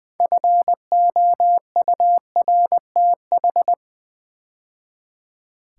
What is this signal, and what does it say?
Morse code, heard as a keyed tone, about 20 wpm, 700 Hz. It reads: FOURTH
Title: 6 Letter Words - First 3 Letters Given - 20wpm